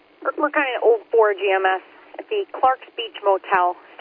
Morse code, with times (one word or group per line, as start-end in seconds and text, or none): none